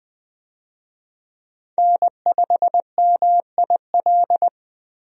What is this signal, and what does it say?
Morse code, heard as a keyed tone, about 21 wpm, 700 Hz. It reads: N5MIL
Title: USA Call Signs - 5 Characters - Repeated 3x - 20wpm